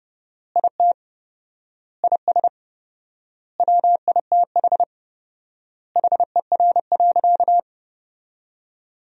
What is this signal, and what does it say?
Morse code, heard as a keyed tone, about 30 wpm, 700 Hz. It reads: IT IS WITH HER.
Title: 4x Word-Spacing - Sentences from Top 100 Words - 30wpm